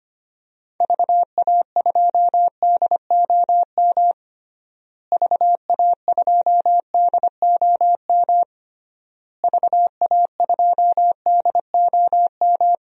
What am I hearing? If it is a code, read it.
Morse code, heard as a keyed tone, about 25 wpm, 700 Hz. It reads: VA2DOM VA2DOM VA2DOM
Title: DXCC-100 Call Signs - 6 Characters - Repeated 3x - 25wpm